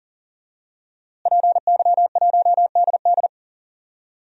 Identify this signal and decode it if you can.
Morse code, heard as a keyed tone, about 40 wpm, 700 Hz. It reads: PY1DD
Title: DXCC-100 Call Signs - 5 Characters - Repeated 3x - 40wpm